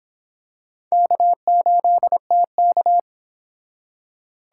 Morse code, heard as a keyed tone, about 26 wpm, 700 Hz. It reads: K8TK